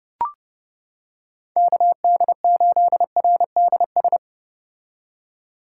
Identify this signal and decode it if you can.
Morse code, heard as a keyed tone, about 30 wpm, 700 Hz. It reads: KD8RDS